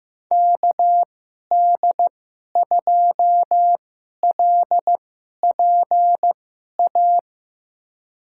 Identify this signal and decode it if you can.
Morse code, heard as a keyed tone, about 15 wpm, spaced as for 11 wpm, 700 Hz. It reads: KD2LPA
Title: USA Call Signs - 6 Characters - 2x Character Spacing - 15wpm